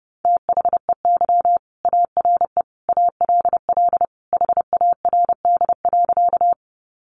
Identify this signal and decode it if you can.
Morse code, heard as a keyed tone, about 30 wpm, 700 Hz. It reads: THEY ARE ALL HARD.
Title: Additional Easy Phrases 30wpm